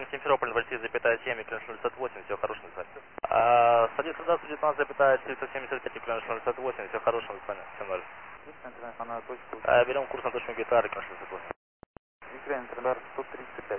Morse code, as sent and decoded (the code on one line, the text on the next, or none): none